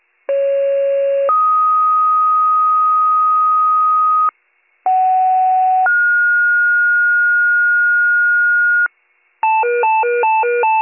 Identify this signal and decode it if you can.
Two-tone page, heard as a one-second tone, then a three-second tone, about 550 then 1250 Hz, about 750 then 1430 Hz, with noise. It is heard from 0.3 to 8.9 s.